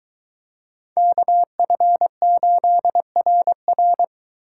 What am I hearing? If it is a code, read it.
Morse code, heard as a keyed tone, about 21 wpm, 700 Hz. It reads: KF8RR